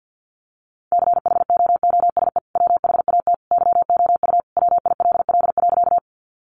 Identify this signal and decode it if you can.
Morse code, heard as a keyed tone, about 48 wpm, 700 Hz. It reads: CHOOSE WHAT YOU WILL.
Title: Sentences from Top 700 Words 50wpm